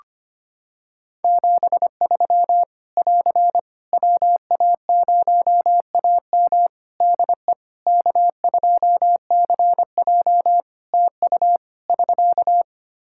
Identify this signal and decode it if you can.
Morse code, heard as a keyed tone, about 25 wpm, 700 Hz. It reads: 73 <AR> WA0AM DE K2CJ TU <SK>
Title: K1USN SST QSO Practice 25wpm